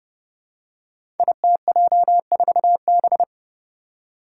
Morse code, heard as a keyed tone, about 30 wpm, 700 Hz. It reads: ITJ4B
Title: Five Challenging Letter-Number Combinations 30wpm